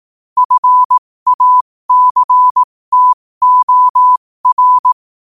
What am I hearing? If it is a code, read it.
Morse code, heard as a keyed tone, about 18 wpm, spaced as for 15 wpm, 1000 Hz. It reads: FACTOR